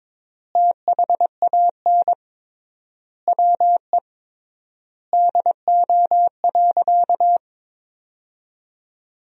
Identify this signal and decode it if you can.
Morse code, heard as a keyed tone, about 23 wpm, 700 Hz. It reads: THAN WE DO.
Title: 3x Word-Spacing - Sentences from Top 100 Words - 22wpm